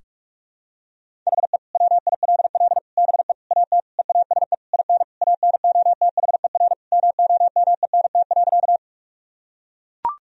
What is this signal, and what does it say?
Morse code, heard as a keyed tone, about 41 wpm, 700 Hz. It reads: HE WILL BE AT EASE IN ANOTHER MOMENT.
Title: Sentences from Top 1000 Words - Review Part 1 - 45wpm